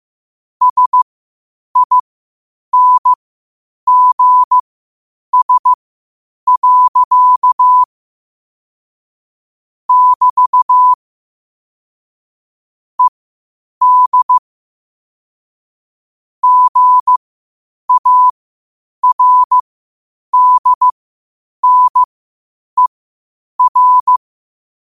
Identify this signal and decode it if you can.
Morse code, heard as a keyed tone, about 15 wpm, spaced as for 8 wpm, 1000 Hz. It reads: SINGS. <BT> ED GARDNER
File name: qotdfiles / 8wpm